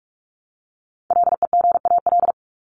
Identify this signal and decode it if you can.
Morse code, heard as a keyed tone, about 44 wpm, 700 Hz. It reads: LEGAL
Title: Top 1001-1250 Words 45wpm